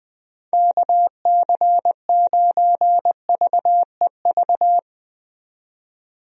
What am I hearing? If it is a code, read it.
Morse code, heard as a keyed tone, about 21 wpm, 700 Hz. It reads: KC9VEV
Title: USA Call Signs - 6 Characters - 20wpm